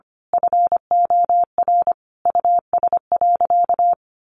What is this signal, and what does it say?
Morse code, heard as a keyed tone, about 25 wpm, 700 Hz. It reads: FOR US.
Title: Sentences from Top 1200 Words 25wpm